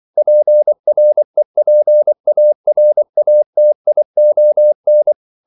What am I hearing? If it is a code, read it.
Morse code, heard as a keyed tone, about 24 wpm, 600 Hz. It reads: PREPARATION
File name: CW / Practice / EnglishQuotations0324